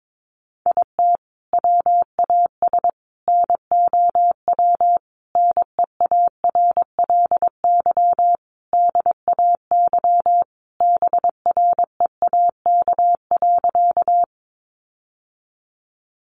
Morse code, heard as a keyed tone, about 22 wpm, 700 Hz. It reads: IT WAS NOW NEARLY DAY BREAK.
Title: Sentences from Top 1300 Words 22wpm